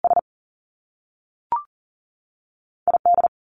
Morse code, heard as a keyed tone, about 41 wpm, 700 Hz. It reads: S ID
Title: Top Two Letter Combinations in Words 40wpm